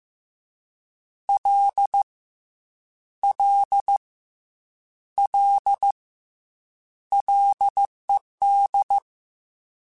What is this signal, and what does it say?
Morse code, heard as a keyed tone, about 15 wpm, 800 Hz. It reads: L L L LED
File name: F and L / F and L mp3 / F and L Prefixes 3x2 Twice @13